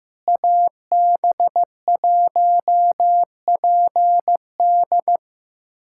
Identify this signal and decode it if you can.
Morse code, heard as a keyed tone, about 15 wpm, 700 Hz. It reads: AB1PD
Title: USA Callsign DE Callsign 15wpm